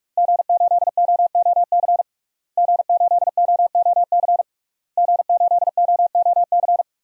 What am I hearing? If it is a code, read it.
Morse code, heard as a keyed tone, about 43 wpm, 700 Hz. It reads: G8OOC G8OOC G8OOC
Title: DXCC-100 Call Signs - 5 Characters - Repeated 3x - 45wpm